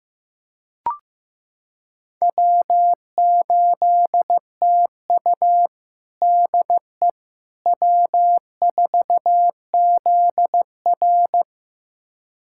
Morse code, heard as a keyed tone, about 15 wpm, 700 Hz. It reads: W8TU DE W4ZR
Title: USA Callsign DE Callsign - 4 Characters - 15wpm